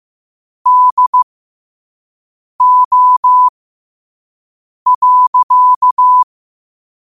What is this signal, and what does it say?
Morse code, heard as a keyed tone, about 15 wpm, spaced as for 5 wpm, 1000 Hz. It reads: DO.
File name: qotdfiles / 5wpm